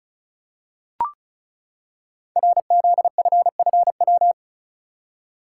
Morse code, heard as a keyed tone, about 36 wpm, 700 Hz. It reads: RZFFW
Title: Code Groups - 5 Letters - 35wpm